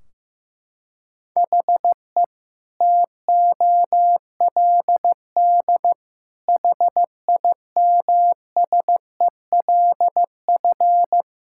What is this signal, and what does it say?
Morse code, heard as a keyed tone, about 15 wpm, 700 Hz. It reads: HE TOLD HIMSELF
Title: Sentences from Top 1600 Words 15wpm